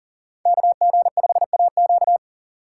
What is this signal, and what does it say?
Morse code, heard as a keyed tone, about 52 wpm, 700 Hz. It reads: KG5AQ